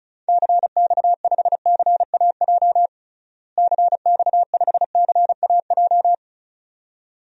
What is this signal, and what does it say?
Morse code, heard as a keyed tone, about 34 wpm, 700 Hz. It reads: CX5CAJ CX5CAJ
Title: DXCC-100 Call Signs - 6 Characters - Repeated 2x - 35wpm